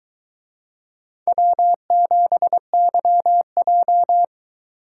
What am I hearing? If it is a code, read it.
Morse code, heard as a keyed tone, about 22 wpm, 700 Hz. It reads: W7YJ